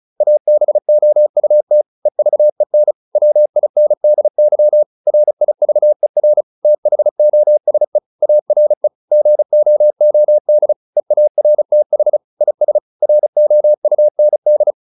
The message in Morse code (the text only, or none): ABOUT EVEN WINDY RIVER THOSE ARE GOOD EARTH IS ROUND